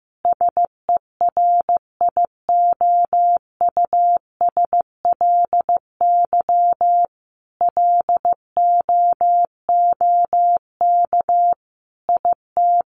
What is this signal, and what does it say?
Morse code, heard as a keyed tone, about 15 wpm, 700 Hz. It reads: SERIOUSLY LOOK IT
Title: Sentences from Top 1200 Words 15wpm